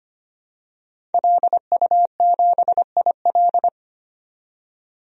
Morse code, heard as a keyed tone, about 25 wpm, 700 Hz. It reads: LU7IL
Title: DXCC-100 Call Signs - 5 Characters - 25wpm